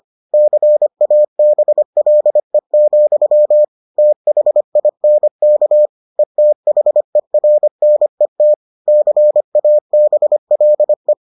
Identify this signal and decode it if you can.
Morse code, heard as a keyed tone, about 25 wpm, 600 Hz. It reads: CABLE, THINK ETHERNET CABLE